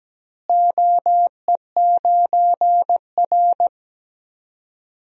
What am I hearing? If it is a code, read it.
Morse code, heard as a keyed tone, about 17 wpm, 700 Hz. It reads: OE9R
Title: DXCC-100 Call Signs - 4 Characters - Repeated 3x - 17wpm